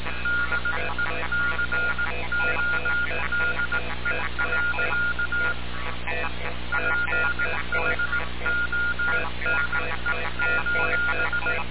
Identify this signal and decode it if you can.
Morse code, heard as a keyed tone, about 18 wpm, 1420 Hz, with noise. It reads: DECODE. PHONE:
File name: radiocap / REC z